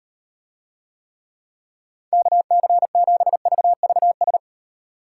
Morse code, heard as a keyed tone, about 38 wpm, 700 Hz. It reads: KC7VVS